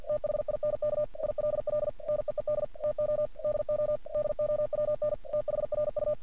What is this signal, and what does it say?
Morse code, heard as a keyed tone, about 50 wpm, 600 Hz. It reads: THINK ILL NEED TO DO DOWN THRU
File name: archived / CRSnet